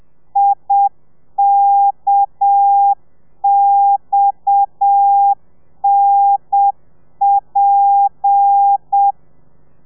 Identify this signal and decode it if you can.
Morse code, heard as a keyed tone, about 7 wpm, 800 Hz. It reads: IKXNP
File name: telegrafia / i2tfj / Lez7 30